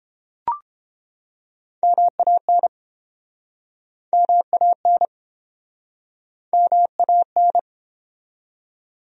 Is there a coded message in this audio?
Yes: Morse code from 1.8 to 7.6 s, heard as a keyed tone, about 33 wpm, 700 Hz. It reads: MAN MAN MAN